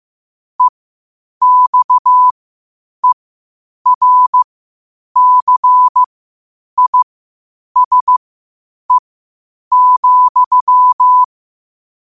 Morse code, heard as a keyed tone, about 15 wpm, spaced as for 8 wpm, 1000 Hz. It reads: EXERCISE,
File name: 8wpm